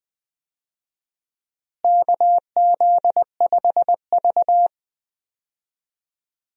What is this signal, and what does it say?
Morse code, heard as a keyed tone, about 20 wpm, 700 Hz. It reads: KZ5V
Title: USA Call Signs - 4 Characters - Repeated 3x - 20wpm